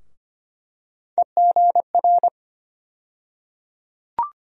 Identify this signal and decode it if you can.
Morse code, heard as a keyed tone, about 25 wpm, 700 Hz. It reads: EGR
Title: USA Call Sign Postfixes - 3 Letters 25wpm